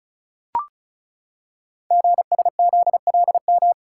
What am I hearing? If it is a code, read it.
Morse code, heard as a keyed tone, about 36 wpm, 700 Hz. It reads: GSZLM
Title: Code Groups - 5 Letters - 35wpm